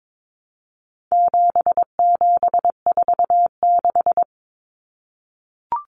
Morse code, heard as a keyed tone, about 22 wpm, 700 Hz. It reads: 7746
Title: Numbers - 4-Digit 22wpm